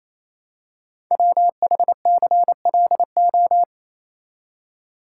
Morse code, heard as a keyed tone, about 29 wpm, 700 Hz. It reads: WHCLO
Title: Code Groups - 5 Letter:Numbers - 28wpm